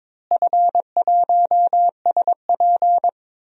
Morse code, heard as a keyed tone, about 22 wpm, 700 Hz. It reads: F1SP
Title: DXCC-100 Call Signs - 4 Characters - Repeated 2x - 22wpm